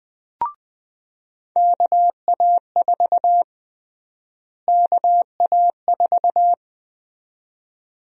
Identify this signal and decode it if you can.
Morse code, heard as a keyed tone, about 20 wpm, 700 Hz. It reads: KA4 KA4
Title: USA Call Sign Prefixes - Repeated 2x - 20wpm